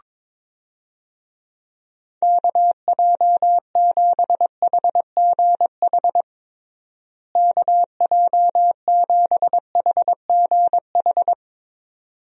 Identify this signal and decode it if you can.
Morse code, heard as a keyed tone, about 22 wpm, 700 Hz. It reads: KJ7HGH KJ7HGH
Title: USA Call Signs - 6 Characters - Repeated 2x - 22wpm